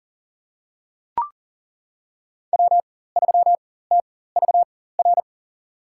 Morse code, heard as a keyed tone, about 40 wpm, spaced as for 19 wpm, 700 Hz. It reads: W3TVR
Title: USA Call Signs - 5 Characters - 4x Character Spacing - 40wpm